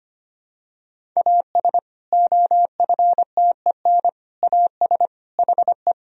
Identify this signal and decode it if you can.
Morse code, heard as a keyed tone, about 25 wpm, 700 Hz. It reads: AS OFTEN AS HE
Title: Sentences from Top 1000 Words - Review Part 1 - 25wpm